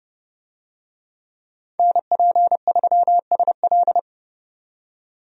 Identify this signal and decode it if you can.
Morse code, heard as a keyed tone, about 31 wpm, 700 Hz. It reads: NP3SL
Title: USA Call Signs - 5 Characters - 30wpm